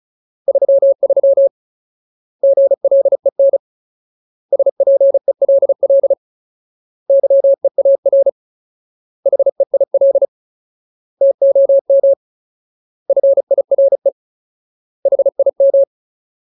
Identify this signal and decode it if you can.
Morse code, heard as a keyed tone, about 34 wpm, 550 Hz. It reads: 33 GLEN SPELL YEAR HEIL TOM FIRE HIM
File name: Ultimate Word List / 35WPM Ultimate